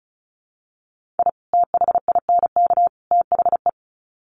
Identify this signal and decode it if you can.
Morse code, heard as a keyed tone, about 34 wpm, 700 Hz. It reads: I THINK THE
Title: Sets of 3 Words 35wpm